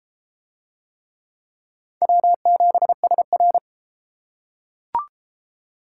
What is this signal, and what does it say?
Morse code, heard as a keyed tone, about 31 wpm, 700 Hz. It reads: W7SR